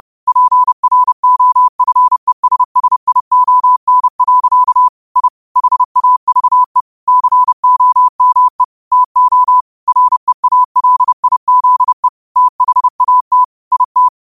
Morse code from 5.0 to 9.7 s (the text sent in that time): I HAVE COME TO